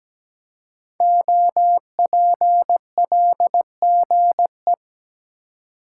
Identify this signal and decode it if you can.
Morse code, heard as a keyed tone, about 17 wpm, 700 Hz. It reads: OPLGE
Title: Code Groups - 5 Letters - 17wpm